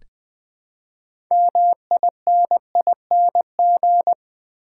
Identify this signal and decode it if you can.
Morse code, heard as a keyed tone, about 21 wpm, 700 Hz. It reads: MINING
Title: Build A Word - 3 Letters At A Time - 6-Letter Words - 20wpm